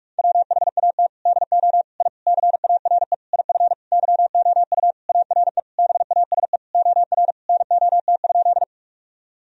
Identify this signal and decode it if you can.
Morse code, heard as a keyed tone, about 47 wpm, 700 Hz. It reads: WHAT DO I CARE IF YOU ARE BASE OR NOT?
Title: Sentences from Top 1000 Words - Review Part 1 - 45wpm